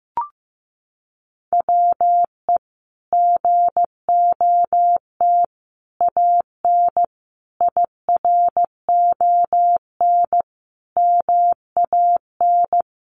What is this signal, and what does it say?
Morse code, heard as a keyed tone, about 15 wpm, 700 Hz. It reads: WE GOT AN IRON MAN
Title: A Sound Of Thunder 15wpm